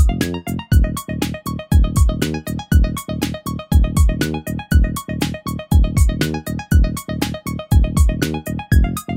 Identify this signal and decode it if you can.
There is no code.